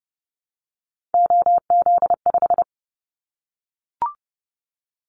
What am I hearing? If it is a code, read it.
Morse code, heard as a keyed tone, about 30 wpm, 700 Hz. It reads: OZ5
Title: Callsign Prefixes for Top 50 DXCC - 30wpm